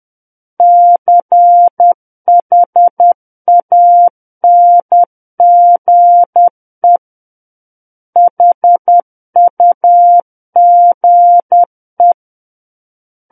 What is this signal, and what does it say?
Morse code, heard as a keyed tone, about 10 wpm, 700 Hz. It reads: CHANGE HUGE
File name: Book One / 01-Chapter 1 - 10 WPM